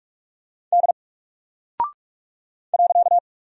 Morse code, heard as a keyed tone, about 46 wpm, 700 Hz. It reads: D .
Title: Single Character (Letter:Number:Punctuation:Prosign) 45wpm